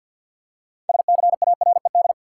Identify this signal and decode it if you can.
Morse code, heard as a keyed tone, about 53 wpm, 700 Hz. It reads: SCARED
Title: Six Letter Words - 50wpm